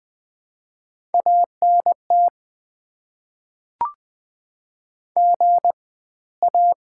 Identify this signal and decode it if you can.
Morse code, heard as a keyed tone, about 20 wpm, spaced as for 15 wpm, 690 Hz. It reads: ANT GA
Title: Top 20 QSO Elements - 4x Character Spacing - 20wpm